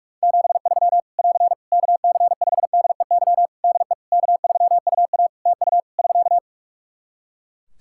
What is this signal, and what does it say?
Morse code, heard as a keyed tone, about 44 wpm, 700 Hz. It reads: 73 <AR> KC5DEY DE K3UA TU <SK>
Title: Practice QSO - Exchange 4 - Ending - 45wpm